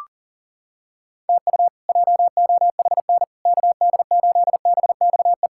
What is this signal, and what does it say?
Morse code, heard as a keyed tone, about 40 wpm, 690 Hz. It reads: TU JOHN KD8BXE
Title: K1USN SST QSO Practice - 40wpm